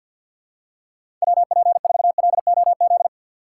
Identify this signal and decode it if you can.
Morse code, heard as a keyed tone, about 52 wpm, 700 Hz. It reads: WP4LOZ
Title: USA Call Signs - 6 Characters - 4x Character Spacing - 50wpm